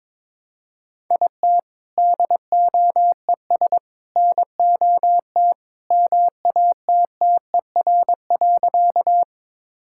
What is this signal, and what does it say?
Morse code, heard as a keyed tone, about 22 wpm, 700 Hz. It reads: IT DOES NOT MATTER.
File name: practice / 22wpm / Sentences with 3-Word Phrases - 22wpm